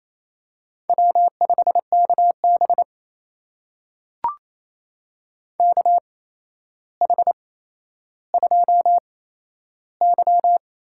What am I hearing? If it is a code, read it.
Morse code, heard as a keyed tone, about 28 wpm, spaced as for 12 wpm, 700 Hz. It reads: W5KB KH2Y
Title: USA Call Signs - 4 Characters - 8x Character Spacing - 28wpm